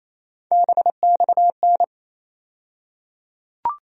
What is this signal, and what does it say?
Morse code, heard as a keyed tone, about 28 wpm, 700 Hz. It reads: BXN
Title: USA Call Sign Postfixes - 3 Letters 28wpm